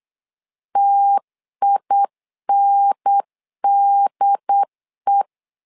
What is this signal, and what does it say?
Morse code, heard as a keyed tone, about 8 wpm, 800 Hz. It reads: TINDE
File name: LEZIONI / 3 MWRO / 3 M 8wpm 2300